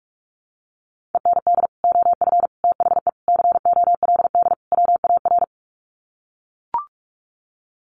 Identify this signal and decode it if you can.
Morse code, heard as a keyed tone, about 44 wpm, 700 Hz. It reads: END OF THE COLD WAR
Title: Sets of 5 Words Encore 45wpm